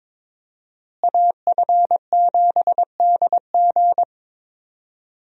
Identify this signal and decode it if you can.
Morse code, heard as a keyed tone, about 22 wpm, 700 Hz. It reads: AF7DG
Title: USA Call Signs - 5 Characters - Repeated 2x - 22wpm